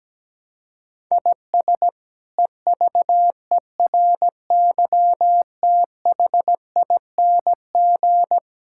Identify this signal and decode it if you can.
Morse code, heard as a keyed tone, about 17 wpm, 700 Hz. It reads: IS EVERYTHING
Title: Sentences from Top 1000 Words - Review Part 6 - 17wpm